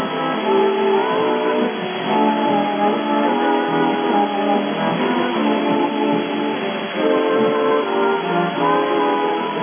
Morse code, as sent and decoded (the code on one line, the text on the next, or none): none